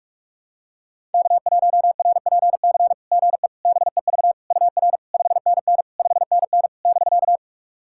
Morse code, heard as a keyed tone, about 44 wpm, 690 Hz. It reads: K1RPC GE BEV UR 5NN 5NN <BK>
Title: SOTA QSO Practice 45wpm